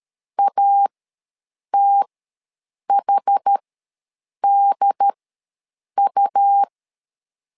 Morse code, heard as a keyed tone, about 13 wpm, spaced as for 7 wpm, 800 Hz. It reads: ATHDU
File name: LEZIONI / 2 UHDN / 2 UHD ws 12wpm 1722